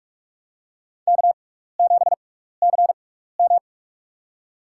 Morse code, heard as a keyed tone, about 46 wpm, spaced as for 16 wpm, 700 Hz. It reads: K7CM
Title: USA Call Signs - 4 Characters - 6x Character Spacing - 45wpm